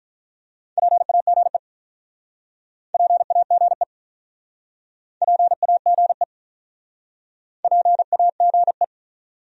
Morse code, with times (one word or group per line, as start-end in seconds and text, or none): 0.8-1.6 PAGE
2.9-3.8 PAGE
5.2-6.2 PAGE
7.6-8.8 PAGE